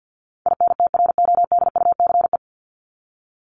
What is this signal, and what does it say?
Morse code, heard as a keyed tone, about 50 wpm, 700 Hz. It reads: INTRODUCE
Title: ED Suffix - Top 100 Words with Base Words 50wpm